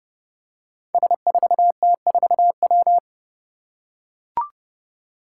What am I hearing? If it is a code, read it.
Morse code, heard as a keyed tone, about 32 wpm, 700 Hz. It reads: S4T4W
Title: Five Challenging Letter-Number Combinations 30wpm